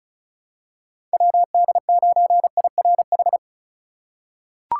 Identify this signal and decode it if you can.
Morse code, heard as a keyed tone, about 36 wpm, 700 Hz. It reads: WD9IRH
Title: USA Call Signs - 6 Characters - 4x Character Spacing - 35wpm